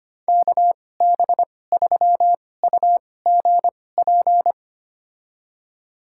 Morse code, heard as a keyed tone, about 25 wpm, spaced as for 18 wpm, 700 Hz. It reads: KB3UGP
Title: USA Call Signs - 6 Characters - 2x Character Spacing - 25wpm